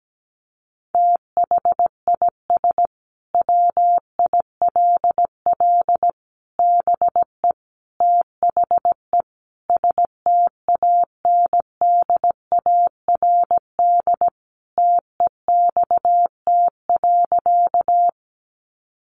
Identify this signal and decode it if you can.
Morse code, heard as a keyed tone, about 17 wpm, 700 Hz. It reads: THIS WILL BE THE STANDARD TEXT.
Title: Sentences from Top 1200 Words 17wpm